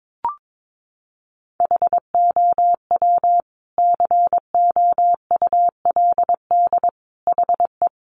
Morse code, heard as a keyed tone, about 22 wpm, 700 Hz. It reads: HOW COULD HE
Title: Sentences from Top 1300 Words 22wpm